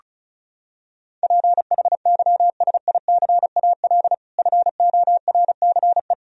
Morse code, heard as a keyed tone, about 34 wpm, 700 Hz. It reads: PHYSICAL FORCE
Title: Sentences from Top 1000 Words - Review Part 4 - 35wpm